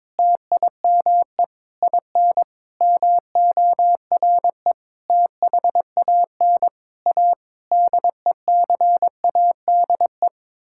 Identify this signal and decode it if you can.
Morse code, heard as a keyed tone, about 22 wpm, 700 Hz. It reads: TIME IN MORE THAN A DECADE
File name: Sets of 9 Words - 22wpm